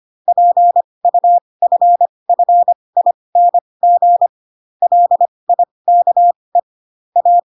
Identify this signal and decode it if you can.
Morse code, heard as a keyed tone, about 25 wpm, spaced as for 20 wpm, 700 Hz. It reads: PUFFING LIKE A
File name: Pinocchio / Pinocchio ch6 15 25 ik1hge